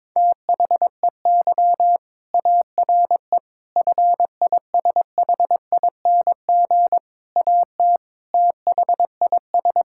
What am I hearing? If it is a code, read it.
Morse code, heard as a keyed tone, about 22 wpm, 700 Hz. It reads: THEY ARE FISHING AT THIS
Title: Sentences from Top 1000 Words - Review Part 5 - 22wpm